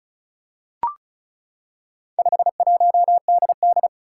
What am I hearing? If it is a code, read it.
Morse code, heard as a keyed tone, about 36 wpm, 700 Hz. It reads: 51DD
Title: Four Challenging Letter-Number Combinations 35wpm